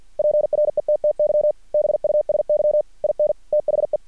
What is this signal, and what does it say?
Morse code, heard as a keyed tone, about 50 wpm, 600 Hz. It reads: PRETTY BUSY IN THE
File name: archived / CRSnet